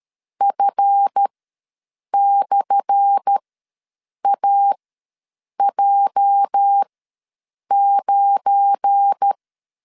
Morse code, heard as a keyed tone, about 13 wpm, spaced as for 7 wpm, 800 Hz. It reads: F/AJ9
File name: LEZIONI / 10 SIGNS / 10C DOT ws 12wpm 3548